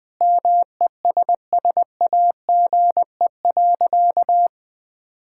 Morse code, heard as a keyed tone, about 20 wpm, 700 Hz. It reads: MESSAGE.
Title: Sentences with 9-Word Phrases - Spoken Beginning - 20wpm